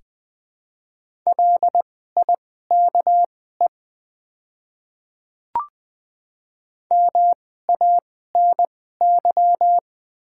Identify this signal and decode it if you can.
Morse code, heard as a keyed tone, about 20 wpm, spaced as for 15 wpm, 700 Hz. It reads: LIKE MANY
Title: 2x Character Spacing - Top 100 Words - 20wpm